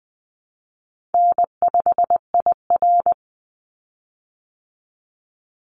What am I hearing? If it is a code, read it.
Morse code, heard as a keyed tone, about 20 wpm, 700 Hz. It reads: N5IR
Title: Call Signs Repeated 3x - 20wpm